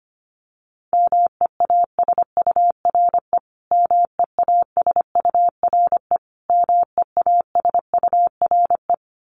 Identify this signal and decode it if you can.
Morse code, heard as a keyed tone, about 25 wpm, 700 Hz. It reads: MEASURE MEASURE MEASURE
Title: Top 500 Words Repeated 25wpm